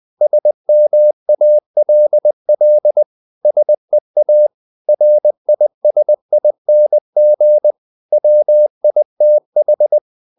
Morse code, heard as a keyed tone, about 20 wpm, 600 Hz. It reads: SMALL SEA RISING WITH
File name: the old man and the sea / the old man and the sea part 1 20wpm